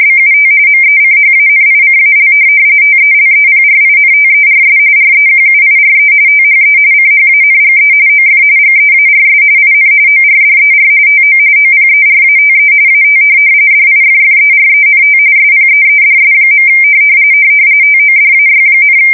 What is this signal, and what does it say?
RTTY, about 45.45 baud. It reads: BUT MECHANICAL TELEPRINTERS REQUIRE THE CHANGE OF GEARS IN ORDER TO OPERATE AT DIFFERENT SPEEDS. TODAY,
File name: fsk170 / teletype-RTTY